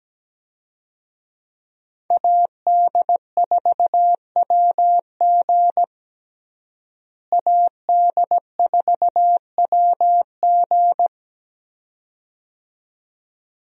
Morse code, heard as a keyed tone, about 17 wpm, 700 Hz. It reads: AD4WG AD4WG
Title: USA Call Signs - 5 Characters - Repeated 2x - 17wpm